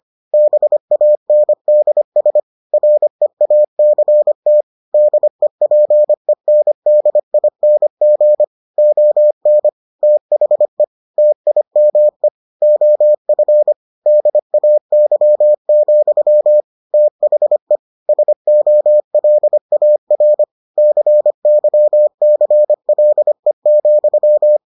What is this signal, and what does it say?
Morse code, heard as a keyed tone, about 25 wpm, 600 Hz. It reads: BANDS REACT DEPENDING ON THE TIME OF DAY, THE SOLAR CYCLE,